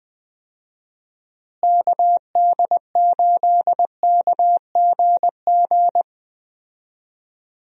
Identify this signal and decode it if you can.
Morse code, heard as a keyed tone, about 20 wpm, 700 Hz. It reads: KD8KGG
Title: USA Call Signs - 6 Characters - Repeated 3x - 20wpm